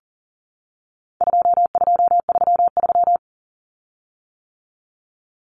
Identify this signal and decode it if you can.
Morse code, heard as a keyed tone, about 40 wpm, 700 Hz. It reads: 2233
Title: Common Cut Numbers - 4-Digit 40wpm